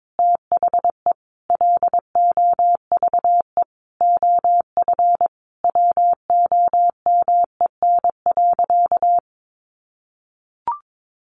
Morse code, heard as a keyed tone, about 22 wpm, 700 Hz. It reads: THE LOVE OF WOMEN.
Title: Sentences from Top 1400 Words 22wpm